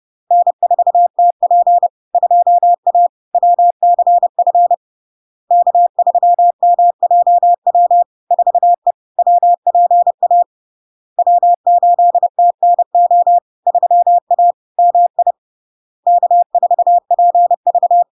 Morse code, heard as a keyed tone, about 30 wpm, 700 Hz. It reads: N4TP 2A WCF K3MJW 4E WPA W8TNO 3A MI K4PV
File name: LARK-FD-2022-Log-in-CW-30wpm-700-Hz